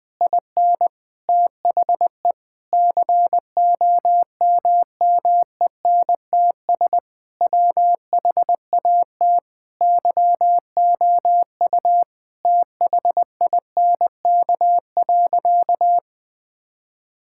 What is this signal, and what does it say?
Morse code, heard as a keyed tone, about 20 wpm, 700 Hz. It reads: IN THE COMMENTS WHAT YOU THINK.
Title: Sentences with 6-Word Phrases - Spoken Beginning - 20wpm